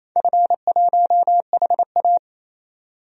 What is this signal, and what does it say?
Morse code, heard as a keyed tone, about 28 wpm, 700 Hz. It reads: F1HA
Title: DXCC-100 Call Signs - 4 Characters - 28wpm